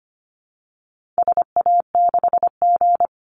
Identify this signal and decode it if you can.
Morse code, heard as a keyed tone, about 25 wpm, 700 Hz. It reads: SA6G
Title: Call Signs - 25wpm